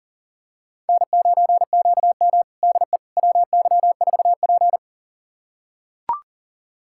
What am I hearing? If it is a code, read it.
Morse code, heard as a keyed tone, about 40 wpm, 700 Hz. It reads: N9QM DE WY4P
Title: USA Callsign DE Callsign - 4 Characters - 40wpm